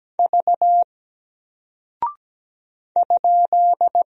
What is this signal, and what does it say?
Morse code, heard as a keyed tone, about 17 wpm, 700 Hz. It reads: V ?
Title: Single Character (Letter:Number:Punctuation:Prosign) Alt 17wpm